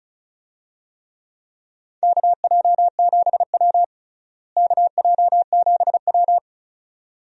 Morse code, heard as a keyed tone, about 36 wpm, 700 Hz. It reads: KJ7W KJ7W
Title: USA Call Signs - 4 Characters - Repeated 2x - 35wpm